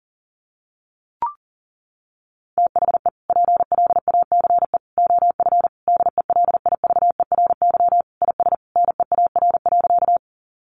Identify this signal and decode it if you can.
Morse code, heard as a keyed tone, about 40 wpm, 700 Hz. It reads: THE PLACE OF DELIVERY IS NEAR.
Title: Sentences from Top 1500 Words 40wpm